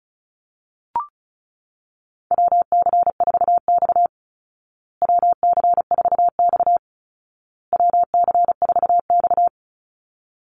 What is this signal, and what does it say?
Morse code, heard as a keyed tone, about 34 wpm, 700 Hz. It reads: WC4X WC4X WC4X
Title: Call Signs Repeated 2x - 35wpm